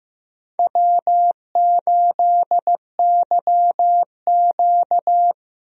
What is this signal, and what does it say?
Morse code, heard as a keyed tone, about 15 wpm, 700 Hz. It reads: W8YQ
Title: USA Call Signs - 4 Characters - Repeated 3x - 15wpm